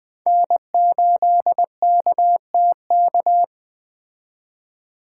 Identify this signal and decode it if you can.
Morse code, heard as a keyed tone, about 21 wpm, 700 Hz. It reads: N8KTK
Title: Speed Racing - USA Call Signs 17wpm